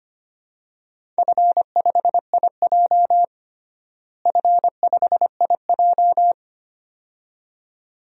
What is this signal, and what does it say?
Morse code, heard as a keyed tone, about 25 wpm, 700 Hz. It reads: F5IJ F5IJ
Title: DXCC-100 Call Signs - 4 Characters - Repeated 2x - 25wpm